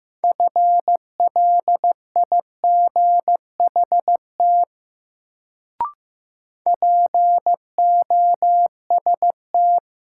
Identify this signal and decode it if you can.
Morse code, heard as a keyed tone, about 15 wpm, 700 Hz. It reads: FLIGHT POST
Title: Sentences from Top 1000 Words - New Word Drill - 15wpm